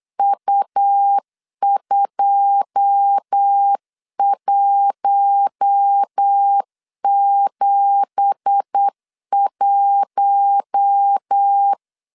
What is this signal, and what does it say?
Morse code, heard as a keyed tone, about 8 wpm, 800 Hz. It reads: U2171
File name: CW / DET / 1 9 1500C S B 8wpm 5624